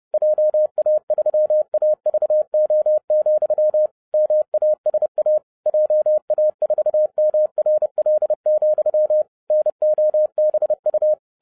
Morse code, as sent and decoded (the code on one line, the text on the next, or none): .--- .- ...-- .- ...- --- --..-- / -- .- ... .- / .--- .- ....- -- .-. .-.. --..-- / -. --- -... ..-
JA3AVO, MASA JA4MRL, NOBU